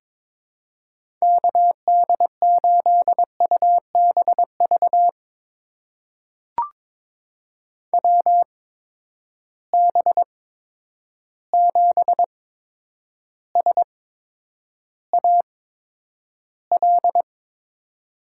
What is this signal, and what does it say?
Morse code, heard as a keyed tone, about 22 wpm, spaced as for 9 wpm, 700 Hz. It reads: KD8UBV WB7SAL
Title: USA Call Signs - 6 Characters - 8x Character Spacing - 22wpm